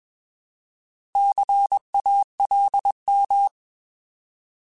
Morse code, heard as a keyed tone, about 21 wpm, 800 Hz. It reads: CALM
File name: Alphabetical / Alphabetical mp3s / Cs @20